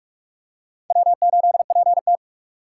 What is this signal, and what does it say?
Morse code, heard as a keyed tone, about 43 wpm, 700 Hz. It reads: W8PT
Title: USA Call Signs - 4x Character Spacing - 45wpm